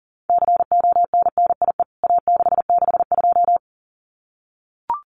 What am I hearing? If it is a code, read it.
Morse code, heard as a keyed tone, about 40 wpm, 700 Hz. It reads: CONNIE A662
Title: CWT Exchange 40wpm